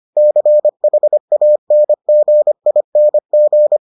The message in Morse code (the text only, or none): CHANGING